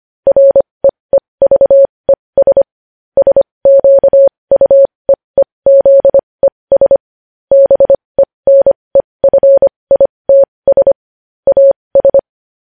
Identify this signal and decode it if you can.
Morse code, heard as a keyed tone, about 25 wpm, spaced as for 20 wpm, 550 Hz. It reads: REEVES SQUEEZES BENEFITS AS
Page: News Headlines in Morse Code at 20 WPM